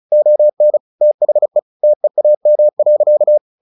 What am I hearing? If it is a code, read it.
Morse code, heard as a keyed tone, about 34 wpm, 600 Hz. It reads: ON THE TEAM.
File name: the old man and the sea / the old man and the sea 1 35wpm